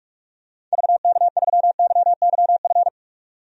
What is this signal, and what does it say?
Morse code, heard as a keyed tone, about 43 wpm, 700 Hz. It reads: VK3YYF
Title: DXCC-100 Call Signs - 6 Characters - 45wpm